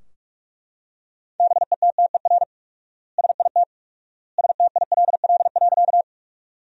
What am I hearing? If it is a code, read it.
Morse code, heard as a keyed tone, about 42 wpm, 700 Hz. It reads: BETTER SIT STILL.
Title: 4x Word-Spacing - Sentences from Top 500 Words - 45wpm